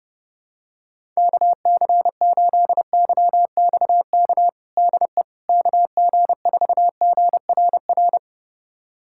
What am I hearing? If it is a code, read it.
Morse code, heard as a keyed tone, about 32 wpm, 700 Hz. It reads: KC8YXK DE KG4GRR
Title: USA Callsign DE Callsign - 6 Characters - 30wpm